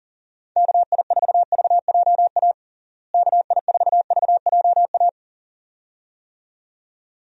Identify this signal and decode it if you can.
Morse code, heard as a keyed tone, about 37 wpm, 700 Hz. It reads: KI4VJA KI4VJA
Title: USA Call Signs - 6 Characters - Repeated 2x - 40wpm